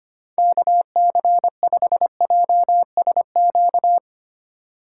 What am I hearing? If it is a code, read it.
Morse code, heard as a keyed tone, about 25 wpm, 700 Hz. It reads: KC5JSQ
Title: USA Call Signs - 6 Characters - 2x Character Spacing - 25wpm